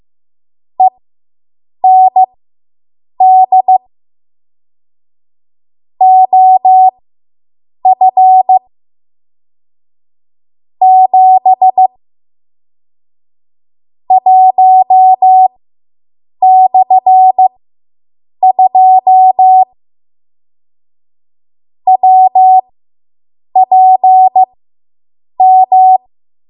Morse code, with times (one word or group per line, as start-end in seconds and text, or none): 0.8-3.8 END
6.0-8.6 OF
10.8-11.9 7
14.1-19.6 1/2
21.9-26.0 WPM